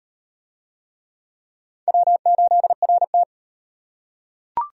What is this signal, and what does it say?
Morse code, heard as a keyed tone, about 37 wpm, 700 Hz. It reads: W8RT